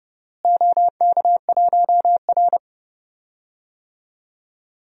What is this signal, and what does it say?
Morse code, heard as a keyed tone, about 31 wpm, 700 Hz. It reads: OK1R
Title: DXCC-100 Call Signs - 4 Characters - Repeated 3x - 30wpm